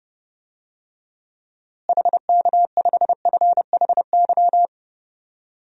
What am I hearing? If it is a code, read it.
Morse code, heard as a keyed tone, about 30 wpm, 700 Hz. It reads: HK5FHY
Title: DXCC-100 Call Signs - 6 Characters - Repeated 2x - 30wpm